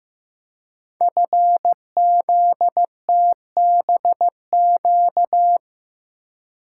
Morse code, heard as a keyed tone, about 15 wpm, 700 Hz. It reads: FZTBQ
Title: Code Groups - 5 Letters - 15wpm